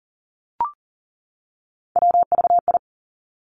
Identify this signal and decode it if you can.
Morse code, heard as a keyed tone, about 42 wpm, 700 Hz. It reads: WVI
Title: Three Challenging Letter-Number Combinations 40wpm